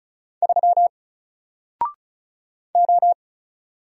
Morse code, heard as a keyed tone, about 36 wpm, 700 Hz. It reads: 3 O
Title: Single Letter-Number 35wpm